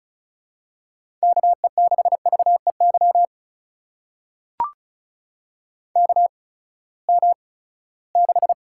Morse code, heard as a keyed tone, about 36 wpm, 700 Hz. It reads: K E 6 V E Y K M 6
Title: USA Call Signs - 6 Characters - 8x Character Spacing - 35wpm